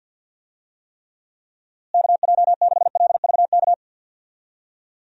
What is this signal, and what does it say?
Morse code, heard as a keyed tone, about 52 wpm, 700 Hz. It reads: KJ6LVK
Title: USA Call Signs - 6 Characters - Repeated 3x - 50wpm